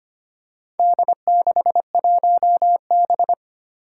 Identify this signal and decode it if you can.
Morse code, heard as a keyed tone, about 25 wpm, 700 Hz. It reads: D61B
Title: Four Challenging Letter-Number Combinations 25wpm